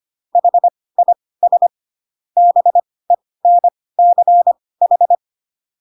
Morse code, heard as a keyed tone, about 25 wpm, spaced as for 18 wpm, 700 Hz. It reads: HIS BENCH
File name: Pinocchio / Pinocchio ch2 12 25 ik1hge